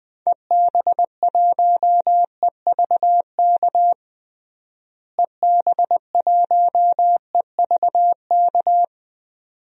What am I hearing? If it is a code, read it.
Morse code, heard as a keyed tone, about 20 wpm, 700 Hz. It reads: EB1EVK EB1EVK
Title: DXCC-100 Call Signs - 6 Characters - Repeated 2x - 20wpm